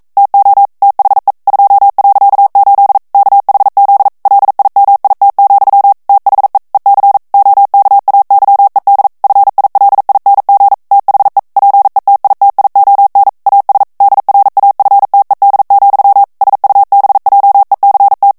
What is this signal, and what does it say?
Morse code, heard as a keyed tone, about 42 wpm, 800 Hz. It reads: TO THE 2.8 KHZ LIMIT, THE EC OKAYED FILING THE PETITION AS DRAFTED, SUBJECT